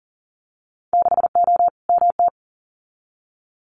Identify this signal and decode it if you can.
Morse code, heard as a keyed tone, about 40 wpm, 700 Hz. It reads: 6O MT
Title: Winter Field Day Exchanges Encore 40wpm